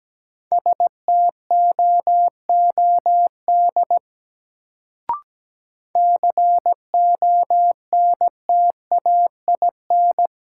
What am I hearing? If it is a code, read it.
Morse code, heard as a keyed tone, about 17 wpm, 700 Hz. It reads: STOOD CONTAIN
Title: Sentences from Top 500 Words - New Word Drill - 17wpm